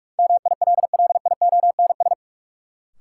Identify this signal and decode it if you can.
Morse code, heard as a keyed tone, about 42 wpm, 700 Hz. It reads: MILLIONS
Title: Sets of 5 Words - New Word Drill - 45wpm